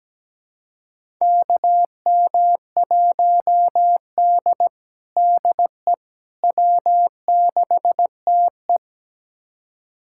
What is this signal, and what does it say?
Morse code, heard as a keyed tone, about 17 wpm, 700 Hz. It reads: KM1D DE W6TE
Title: USA Callsign DE Callsign - 4 Characters - 17wpm